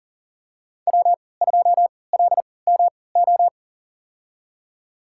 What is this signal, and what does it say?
Morse code, heard as a keyed tone, about 40 wpm, spaced as for 23 wpm, 700 Hz. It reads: W2LMO
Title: USA Call Signs - 5 Characters - 3x Character Spacing - 40wpm